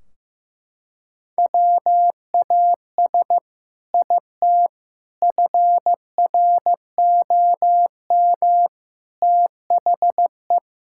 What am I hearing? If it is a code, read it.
Morse code, heard as a keyed tone, about 16 wpm, 700 Hz. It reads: WAS IT FROM THE
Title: Sentences from Top 1600 Words 15wpm